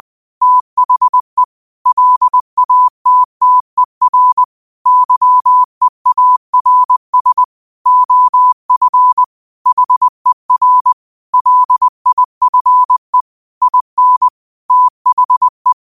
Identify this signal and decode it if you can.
Morse code, heard as a keyed tone, about 20 wpm, 1000 Hz. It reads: THE LATTER YEARS OF HER LIFE IN THE
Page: New quotes every day in morse code at 20 Words per minute.